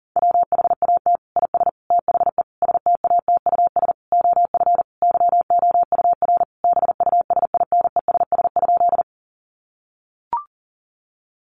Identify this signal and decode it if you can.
Morse code, heard as a keyed tone, about 40 wpm, 700 Hz. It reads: WHAT IS THE STATUS OF YOUR BUSINESS?
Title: Sentences from Top 1400 Words 40wpm